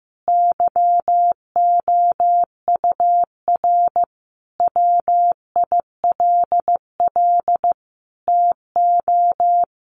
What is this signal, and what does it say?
Morse code, heard as a keyed tone, about 15 wpm, 700 Hz. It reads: YOUR WILL TO
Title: Sentences from Top 1500 Words 15wpm